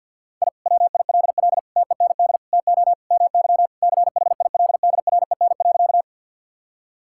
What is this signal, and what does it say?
Morse code, heard as a keyed tone, about 50 wpm, 690 Hz. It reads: I WILL TEND TO MY CHILDREN.
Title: Sentences from Top 1000 Words - Review Part 5 - 50wpm